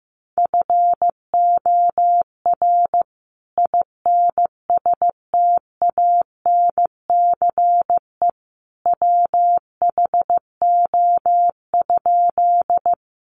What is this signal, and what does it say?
Morse code, heard as a keyed tone, about 15 wpm, 700 Hz. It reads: FOR INSTANCE WHO?
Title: Sentences from Top 800 Words 15wpm